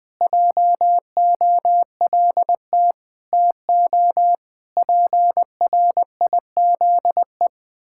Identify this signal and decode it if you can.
Morse code, heard as a keyed tone, about 20 wpm, 700 Hz. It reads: JOLT TO PRIZE
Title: Pangram Challenge v2 - 20wpm